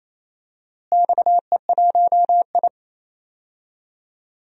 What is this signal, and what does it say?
Morse code, heard as a keyed tone, about 28 wpm, 700 Hz. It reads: XE1I
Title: DXCC-100 Call Signs - 4 Characters - Repeated 3x - 28wpm